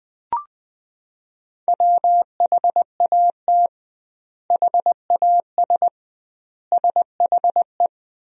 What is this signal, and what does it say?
Morse code, heard as a keyed tone, about 21 wpm, 700 Hz. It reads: WHAT HAS SHE
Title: 2x Word-Spacing - Sentences from Top 100 Words - 20wpm